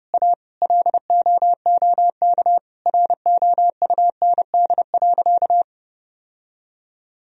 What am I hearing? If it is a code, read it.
Morse code, heard as a keyed tone, about 30 wpm, 700 Hz. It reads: A LOOK ROUND.
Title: Sentences from Top 1000 Words - Review Part 1 - 30wpm